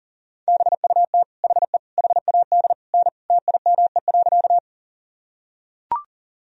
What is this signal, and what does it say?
Morse code, heard as a keyed tone, about 37 wpm, 700 Hz. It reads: BUT HE HAD N TIME.
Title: Sentences from Top 100 Words and Abbreviations 40wpm